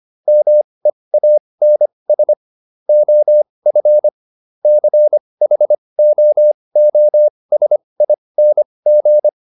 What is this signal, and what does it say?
Morse code, heard as a keyed tone, about 25 wpm, spaced as for 20 wpm, 600 Hz. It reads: MEANS OF CHOOSING